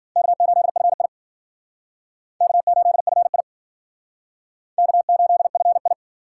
Morse code, heard as a keyed tone, about 53 wpm, 700 Hz. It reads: K8FI K8FI K8FI